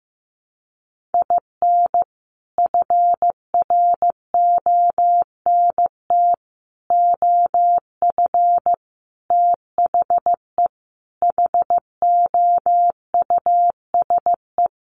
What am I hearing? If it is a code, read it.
Morse code, heard as a keyed tone, about 15 wpm, 700 Hz. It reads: IN FRONT OF THE HOUSE
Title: Sets of 5 Words Encore2 15wpm